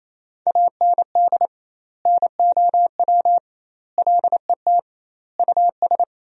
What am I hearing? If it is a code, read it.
Morse code, heard as a keyed tone, about 28 wpm, 700 Hz. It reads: AND NOW LET US
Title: 2x Word-Spacing - Sentences from Top 300 Words - 28wpm